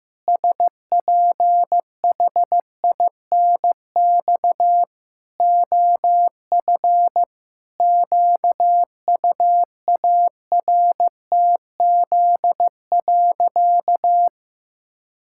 Morse code, heard as a keyed tone, about 15 wpm, 700 Hz. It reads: SPHINX OF QUARTZ.
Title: Pangram Challenge v2 - 15wpm